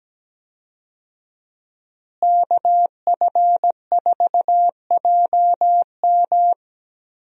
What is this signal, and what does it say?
Morse code, heard as a keyed tone, about 17 wpm, 700 Hz. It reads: KF4JM
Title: USA Call Signs - 5 Characters - Repeated 3x - 17wpm